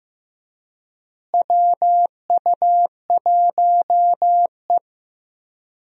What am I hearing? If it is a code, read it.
Morse code, heard as a keyed tone, about 16 wpm, 700 Hz. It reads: WU1E
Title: USA Call Signs - 4 Characters - Repeated 3x - 15wpm